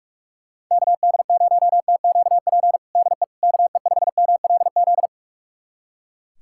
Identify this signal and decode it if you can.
Morse code, heard as a keyed tone, about 41 wpm, 700 Hz. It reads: KD0TQP DE KE5MLZ
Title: USA Callsign DE Callsign - 6 Characters - 45wpm